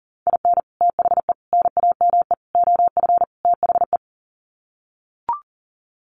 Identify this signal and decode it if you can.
Morse code, heard as a keyed tone, about 40 wpm, 700 Hz. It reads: IN THE NAME OF THE
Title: Sets of 5 Words Encore 40wpm